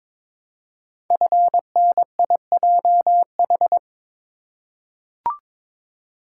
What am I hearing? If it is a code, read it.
Morse code, heard as a keyed tone, about 22 wpm, 700 Hz. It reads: FNIJH
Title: Code Groups - 5 Letters - 22wpm